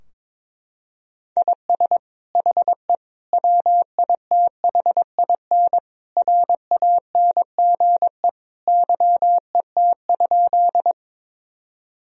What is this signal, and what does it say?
Morse code, heard as a keyed tone, about 22 wpm, 700 Hz. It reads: IS HE WITHIN RANGE YET?
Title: Sentences from Top 1000 Words - Review Part 4 - 22wpm